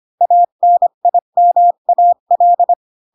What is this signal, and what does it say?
Morse code, heard as a keyed tone, about 25 wpm, 700 Hz. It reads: ANIMAL
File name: Pinocchio / Pinocchio ch7 20 25 ik1hge